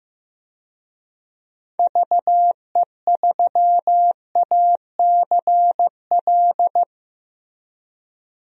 Morse code, heard as a keyed tone, about 15 wpm, 700 Hz. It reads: VE3ACL
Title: DXCC-100 Call Signs - 6 Characters - Repeated 3x - 15wpm